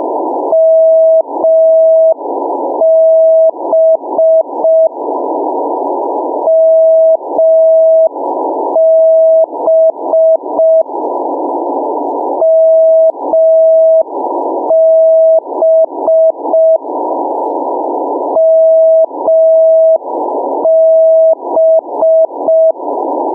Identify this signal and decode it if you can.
Morse code, heard as a keyed tone, about 5 wpm, 650 Hz. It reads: MB MB MB MB
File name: mw / 0437 499 mb beacon